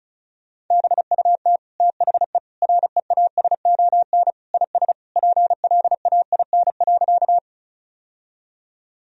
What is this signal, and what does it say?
Morse code, heard as a keyed tone, about 36 wpm, 700 Hz. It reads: BUT THE REASON IS PLAIN.
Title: Sentences from Top 1000 Words - Review Part 2 - 35wpm